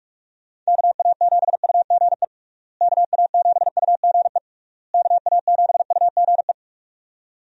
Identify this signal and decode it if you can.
Morse code, heard as a keyed tone, about 46 wpm, 700 Hz. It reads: KA7UGE KA7UGE KA7UGE
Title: USA Call Signs - 6 Characters - Repeated 3x - 45wpm